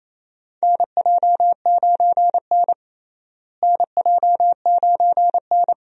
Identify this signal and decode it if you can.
Morse code, heard as a keyed tone, about 28 wpm, 700 Hz. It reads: NJ9N NJ9N
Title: USA Call Signs - 4 Characters - Repeated 2x - 28wpm